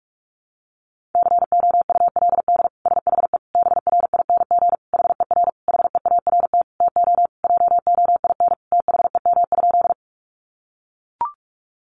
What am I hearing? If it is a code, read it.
Morse code, heard as a keyed tone, about 44 wpm, 700 Hz. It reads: COULD SHE BRING HER HEART TO JOIN THEM?
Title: Sentences from Top 1400 Words 45wpm